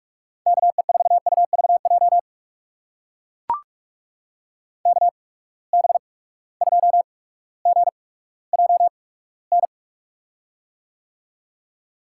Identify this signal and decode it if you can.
Morse code, heard as a keyed tone, about 44 wpm, spaced as for 19 wpm, 700 Hz. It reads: KE4UVJ KB2GJN
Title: USA Call Signs - 6 Characters - 8x Character Spacing - 45wpm